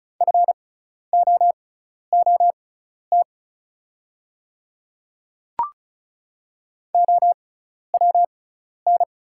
Morse code, heard as a keyed tone, about 36 wpm, spaced as for 12 wpm, 700 Hz. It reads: FOOT OWN
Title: 6x Character Spacing - Top 500 Words - 35wpm